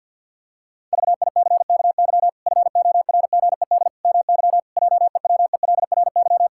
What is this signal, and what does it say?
Morse code, heard as a keyed tone, about 48 wpm, 700 Hz. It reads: VICKY FORGED MY JEWELRY
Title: Pangram Challenge v2 - 50wpm